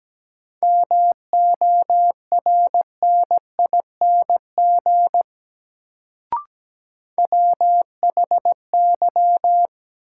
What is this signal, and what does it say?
Morse code, heard as a keyed tone, about 17 wpm, 700 Hz. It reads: MORNING WHY
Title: Sets of 3 Words - New Word Drill - 17wpm